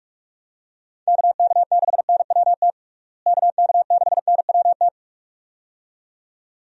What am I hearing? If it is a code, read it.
Morse code, heard as a keyed tone, about 41 wpm, 700 Hz. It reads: KK6NWT KK6NWT
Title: USA Call Signs - 6 Characters - Repeated 2x - 45wpm